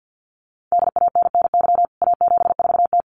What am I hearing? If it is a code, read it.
Morse code, heard as a keyed tone, about 50 wpm, 700 Hz. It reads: DANNY A74T
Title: CWT Exchange 50wpm